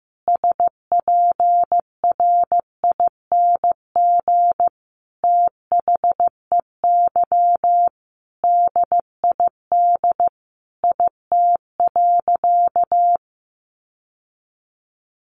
Morse code, heard as a keyed tone, about 15 wpm, 700 Hz. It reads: SPRING THEY DID IT.
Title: Sentences from Top 1200 Words 15wpm